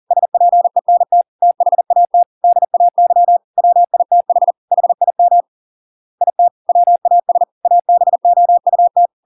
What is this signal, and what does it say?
Morse code, heard as a keyed tone, about 40 wpm, 700 Hz. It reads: SPENT THAT DAY WITH HIM IT WAS ABOUT